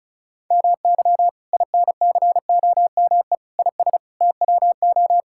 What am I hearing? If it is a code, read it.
Morse code, heard as a keyed tone, about 34 wpm, 700 Hz. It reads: MY INCOME IS TWO
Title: Sentences from Top 1000 Words - Review Part 5 - 35wpm